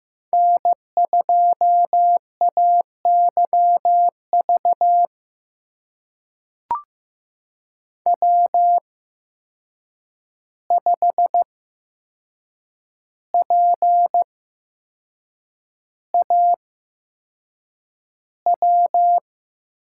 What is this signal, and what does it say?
Morse code, heard as a keyed tone, about 15 wpm, spaced as for 6 wpm, 700 Hz. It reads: N2AYV W5PAW
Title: USA Call Signs - 5 Characters - 8x Character Spacing - 15wpm